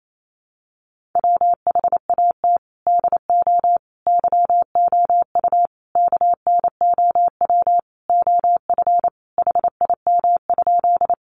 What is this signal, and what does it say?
Morse code, heard as a keyed tone, about 28 wpm, 700 Hz. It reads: WHAT DO YOU KNOW OF HIM?
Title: Sentences from Top 100 Words 28wpm